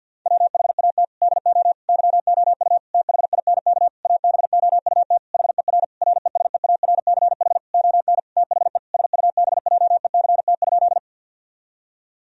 Morse code, heard as a keyed tone, about 50 wpm, 700 Hz. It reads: WHAT DO YOU THINK ABOUT HER RESEARCH ON THE SUBJECT?
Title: Sentences from Top 1000 Words - Review Part 4 - 50wpm